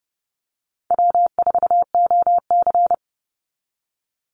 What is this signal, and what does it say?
Morse code, heard as a keyed tone, about 30 wpm, 700 Hz. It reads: W4OC
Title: Easy Call Signs Repeated 2x - 30wpm